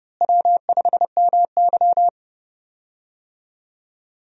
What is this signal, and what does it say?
Morse code, heard as a keyed tone, about 30 wpm, 700 Hz. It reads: W5MY
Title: USA Call Signs - 4 Characters - Repeated 3x - 30wpm